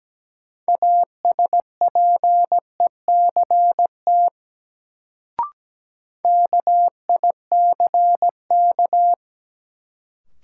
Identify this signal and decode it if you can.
Morse code, heard as a keyed tone, about 17 wpm, 700 Hz. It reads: ASPECT KICK
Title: Sentences from Top 1300 Words - New Word Drill - 17wpm